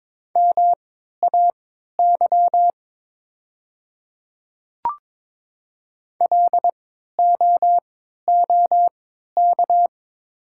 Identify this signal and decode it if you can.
Morse code, heard as a keyed tone, about 22 wpm, spaced as for 12 wpm, 700 Hz. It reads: MAY LOOK
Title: 3x Character Spacing - Top 100 Words - 22wpm